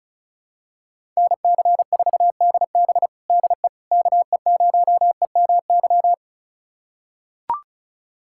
Morse code, heard as a keyed tone, about 36 wpm, 700 Hz. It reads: NC4DB DE KE0EMY
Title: USA Callsign DE Callsign 35wpm